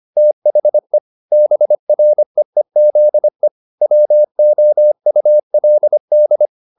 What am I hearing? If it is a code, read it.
Morse code, heard as a keyed tone, about 25 wpm, 600 Hz. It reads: THE BREEZE WOULD